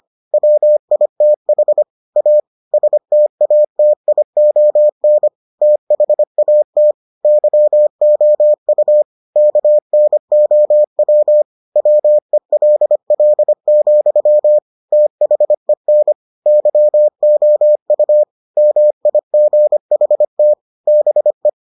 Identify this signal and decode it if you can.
Morse code, heard as a keyed tone, about 25 wpm, 600 Hz. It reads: WITH A STATION THAT YOU KNOW WELL, THEN YOU MIGHT BE